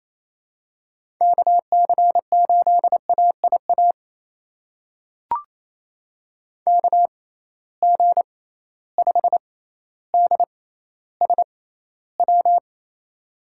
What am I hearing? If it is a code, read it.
Morse code, heard as a keyed tone, about 28 wpm, spaced as for 14 wpm, 700 Hz. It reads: KC8AIA KG5DSW
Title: USA Call Signs - 6 Characters - 6x Character Spacing - 28wpm